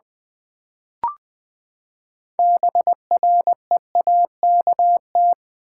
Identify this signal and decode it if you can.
Morse code, heard as a keyed tone, about 20 wpm, 700 Hz. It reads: BREAKT
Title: Build A Word - 6 Letters At A Time - 12-Letter Words - 20wpm